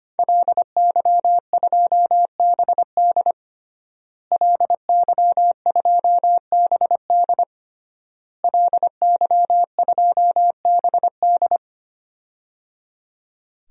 Morse code, heard as a keyed tone, about 25 wpm, 700 Hz. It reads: LY2BD LY2BD LY2BD
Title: DXCC-100 Call Signs - 5 Characters - Repeated 3x - 25wpm